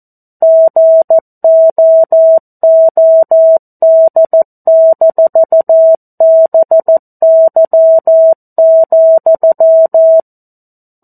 Morse code, heard as a keyed tone, about 14 wpm, 650 Hz. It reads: GOOD-BY,
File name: 02-Chapter-02 14 WPM